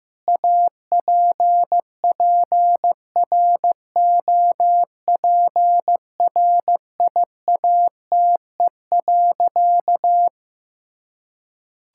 Morse code, heard as a keyed tone, about 16 wpm, 700 Hz. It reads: APPROPRIATE.
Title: Sentences from Top 1000 Words - Review Part 5 - 15wpm